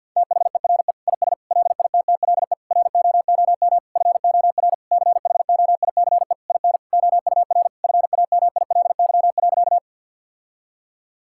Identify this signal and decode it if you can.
Morse code, heard as a keyed tone, about 50 wpm, 700 Hz. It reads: THERE IS LITTLE ROOM FOR CHOICE IN OUR FAMILY.
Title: Sentences from Top 1000 Words - Review Part 6 - 50wpm